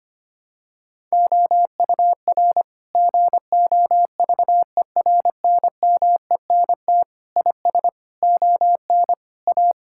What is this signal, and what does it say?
Morse code, heard as a keyed tone, about 25 wpm, 700 Hz. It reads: OUR GOVERNMENT IS ON A
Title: Sentences from Top 1000 Words - Review Part 5 - 25wpm